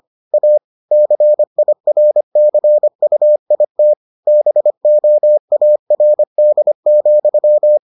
Morse code, heard as a keyed tone, about 25 wpm, 600 Hz. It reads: A CIRCUIT BOARD,